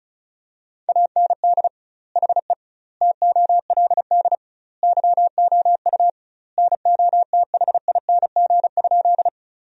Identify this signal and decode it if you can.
Morse code, heard as a keyed tone, about 36 wpm, 700 Hz. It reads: AND HE TOLD YOU NOTHING?
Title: 2x Word-Spacing - Sentences from Top 500 Words - 35wpm